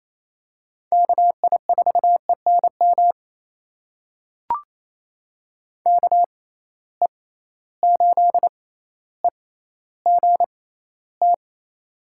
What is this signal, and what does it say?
Morse code, heard as a keyed tone, about 28 wpm, spaced as for 14 wpm, 700 Hz. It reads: KI4ENM KE8EGT
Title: USA Call Signs - 6 Characters - 6x Character Spacing - 28wpm